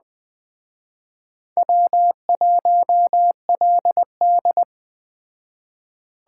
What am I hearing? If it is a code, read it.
Morse code, heard as a keyed tone, about 20 wpm, 700 Hz. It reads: W1LD
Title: Speed Racing - USA Call Signs 17wpm